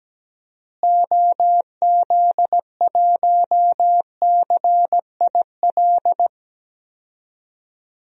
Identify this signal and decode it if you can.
Morse code, heard as a keyed tone, about 17 wpm, 700 Hz. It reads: OZ1CIL
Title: DXCC-100 Call Signs - 6 Characters - Repeated 2x - 17wpm